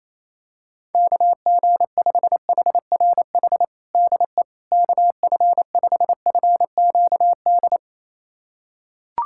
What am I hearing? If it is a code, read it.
Morse code, heard as a keyed tone, about 28 wpm, 700 Hz. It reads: KG5HRH DE KF5FQD
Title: USA Callsign DE Callsign - 6 Characters - 28wpm